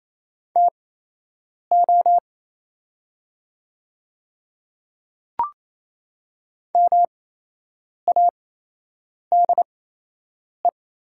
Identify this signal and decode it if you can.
Morse code, heard as a keyed tone, about 27 wpm, spaced as for 8 wpm, 700 Hz. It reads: TO MADE
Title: 8x Character Spacing - Top 200 Words - 28wpm